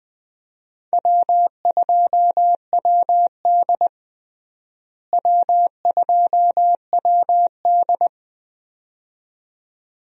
Morse code, heard as a keyed tone, about 20 wpm, 700 Hz. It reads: W2WD W2WD
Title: USA Call Signs - 4 Characters - Repeated 2x - 20wpm